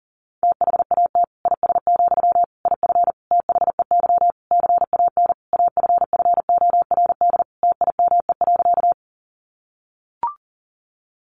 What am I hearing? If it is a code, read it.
Morse code, heard as a keyed tone, about 40 wpm, 700 Hz. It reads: THAT IS, IF THEY CAN AFFORD TIME.
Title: Sentences from Top 1100 Words 40wpm